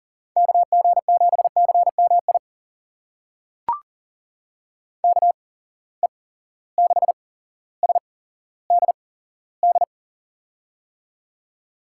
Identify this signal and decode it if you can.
Morse code, heard as a keyed tone, about 40 wpm, spaced as for 17 wpm, 700 Hz. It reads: KG7CMI KE6SDD
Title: USA Call Signs - 6 Characters - 8x Character Spacing - 40wpm